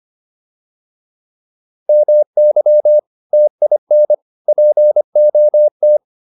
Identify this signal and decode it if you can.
Morse code, heard as a keyed tone, about 25 wpm, 600 Hz. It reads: MY TIN POT